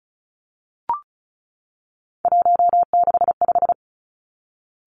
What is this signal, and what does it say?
Morse code, heard as a keyed tone, about 36 wpm, 700 Hz. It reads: 165
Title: Three Challenging Letter-Number Combinations 35wpm